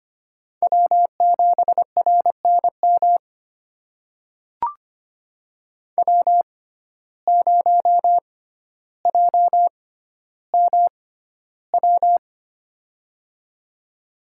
Morse code, heard as a keyed tone, about 25 wpm, spaced as for 13 wpm, 700 Hz. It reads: W7RNM W0JMW
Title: USA Call Signs - 5 Characters - 6x Character Spacing - 25wpm